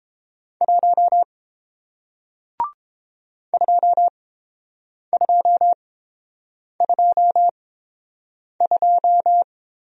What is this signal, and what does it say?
Morse code, heard as a keyed tone, about 30 wpm, spaced as for 7 wpm, 700 Hz. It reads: 1 2222